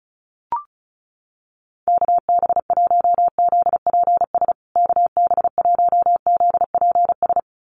Morse code, heard as a keyed tone, about 34 wpm, 700 Hz. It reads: KB1ZPS KB1ZPS
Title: USA Call Signs Repeated 2x 35wpm